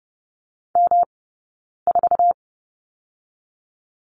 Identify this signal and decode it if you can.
Morse code, heard as a keyed tone, about 30 wpm, 700 Hz. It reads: M 4
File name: Lesson 20d - Number 3 - Two Characters 30wpm